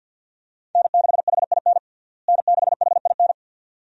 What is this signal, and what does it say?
Morse code, heard as a keyed tone, about 50 wpm, 700 Hz. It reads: N6HIN N6HIN
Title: USA Call Signs - 5 Characters - Repeated 2x - 50wpm